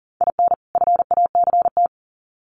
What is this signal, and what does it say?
Morse code, heard as a keyed tone, about 40 wpm, 700 Hz. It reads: IN FACT
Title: Sets of 2 Words 40wpm